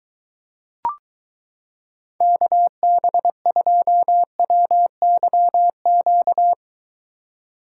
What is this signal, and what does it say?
Morse code, heard as a keyed tone, about 22 wpm, 700 Hz. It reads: KB2WYQ